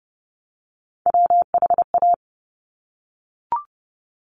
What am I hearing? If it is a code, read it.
Morse code, heard as a keyed tone, about 30 wpm, 700 Hz. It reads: WHA
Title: Top Three Letter Combinations in Words 30wpm